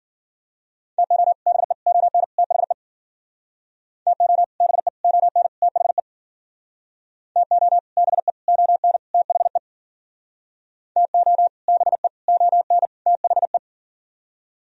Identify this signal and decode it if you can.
Morse code, heard as a keyed tone, about 51 wpm, 700 Hz. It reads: TO BE ON THE TO BE ON THE TO BE ON THE TO BE ON THE